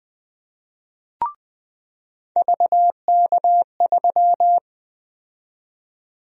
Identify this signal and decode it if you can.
Morse code, heard as a keyed tone, about 20 wpm, 700 Hz. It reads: VK3
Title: USA Call Sign Prefixes - 20wpm